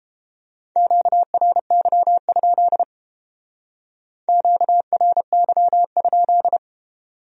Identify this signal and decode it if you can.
Morse code, heard as a keyed tone, about 30 wpm, 700 Hz. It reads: QRY? QRY?